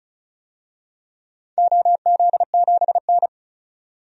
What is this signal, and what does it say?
Morse code, heard as a keyed tone, about 36 wpm, 700 Hz. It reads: OZ7N
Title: DXCC-100 Call Signs - 4 Characters - 35wpm